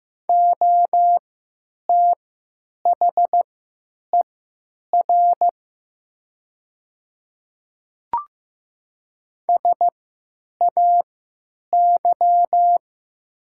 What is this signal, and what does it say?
Morse code, heard as a keyed tone, about 15 wpm, spaced as for 9 wpm, 700 Hz. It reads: OTHER SAY
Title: 3x Character Spacing - Top 200 Words - 15wpm